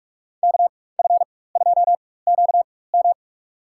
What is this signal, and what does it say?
Morse code, heard as a keyed tone, about 47 wpm, spaced as for 21 wpm, 700 Hz. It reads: KF2QM
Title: USA Call Signs - 5 Characters - 4x Character Spacing - 45wpm